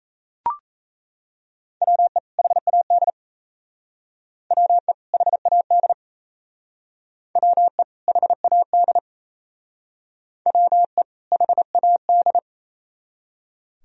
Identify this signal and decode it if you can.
Morse code, heard as a keyed tone, about 35 wpm, 700 Hz. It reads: WE HAD WE HAD WE HAD WE HAD
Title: Speed-Racing - Sets of 2 Words 28wpm